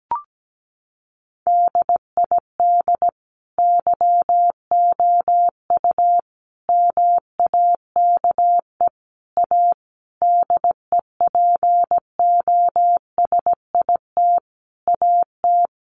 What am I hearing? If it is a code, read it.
Morse code, heard as a keyed tone, about 17 wpm, 700 Hz. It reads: DID YOU MAKE A DEPOSIT AT
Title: Sentences from Top 1300 Words 17wpm